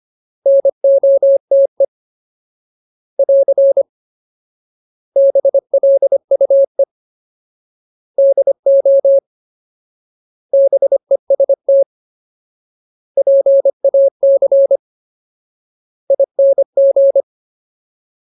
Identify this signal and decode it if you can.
Morse code, heard as a keyed tone, about 25 wpm, 550 Hz. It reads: NOTE <AR> BLUE DO BEST PAC ING
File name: Ultimate Word List / 25WPM Ultimate